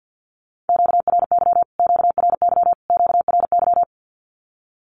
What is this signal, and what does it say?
Morse code, heard as a keyed tone, about 50 wpm, 700 Hz. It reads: QRY QRY QRY
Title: Q-Codes Repeated 3x 50wpm